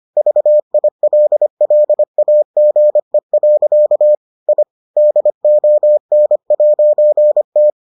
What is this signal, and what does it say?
Morse code, heard as a keyed tone, about 25 wpm, 600 Hz. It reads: VILLAGE. I DON'T